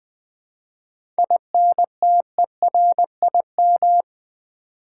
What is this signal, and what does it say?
Morse code, heard as a keyed tone, about 20 wpm, 700 Hz. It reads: INTERIM
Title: 7 Letter Words - First 3 Letters Given - 20wpm